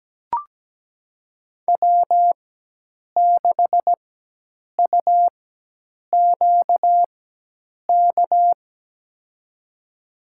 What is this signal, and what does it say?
Morse code, heard as a keyed tone, about 17 wpm, spaced as for 8 wpm, 700 Hz. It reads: W6UQK
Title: USA Call Signs - 4x Character Spacing - 17wpm